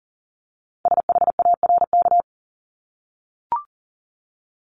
Morse code, heard as a keyed tone, about 39 wpm, 700 Hz. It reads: SHARK
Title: Multi-letter Phonemes 40wpm